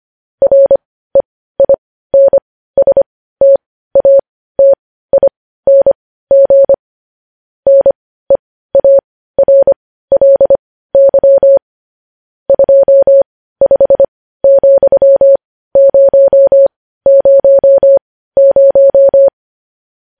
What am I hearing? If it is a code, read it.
Morse code, heard as a keyed tone, about 25 wpm, spaced as for 15 wpm, 550 Hz. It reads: REINSTATING NEARLY 25,000
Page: news-headlines-in-morse-code-at-15-wpm